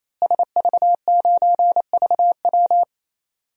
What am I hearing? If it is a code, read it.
Morse code, heard as a keyed tone, about 28 wpm, 700 Hz. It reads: SV9VW
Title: DXCC-100 Call Signs - 5 Characters - Repeated 3x - 28wpm